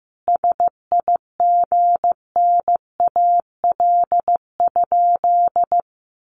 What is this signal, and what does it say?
Morse code, heard as a keyed tone, about 15 wpm, 700 Hz. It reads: SIGNAL?
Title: Sentences from Top 1400 Words 15wpm